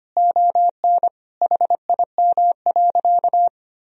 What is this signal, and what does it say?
Morse code, heard as a keyed tone, about 25 wpm, 700 Hz. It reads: ON HIM.
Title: Sentences from Top 1000 Words - Review Part 3 - 25wpm